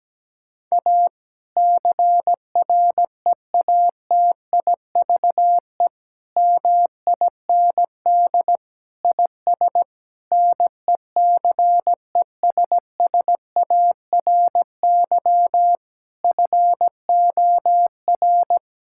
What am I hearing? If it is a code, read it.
Morse code, heard as a keyed tone, about 17 wpm, 700 Hz. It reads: A CREATIVE MIND IS NECESSARY FOR
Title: Sentences from Top 1000 Words - Review Part 5 - 17wpm